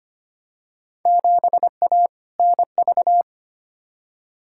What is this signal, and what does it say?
Morse code, heard as a keyed tone, about 25 wpm, 700 Hz. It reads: 7A NV
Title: Field Day Exchanges Encore - 25wpm